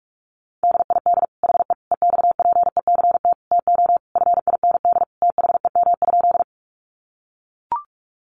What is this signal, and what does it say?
Morse code, heard as a keyed tone, about 44 wpm, 700 Hz. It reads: DID HE EXPECT TO FIND THEM?
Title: Sentences from Top 1100 Words 45wpm